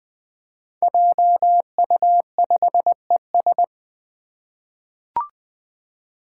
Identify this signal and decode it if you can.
Morse code, heard as a keyed tone, about 20 wpm, 700 Hz. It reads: JU5ES
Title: Five Challenging Letter-Number Combinations 20wpm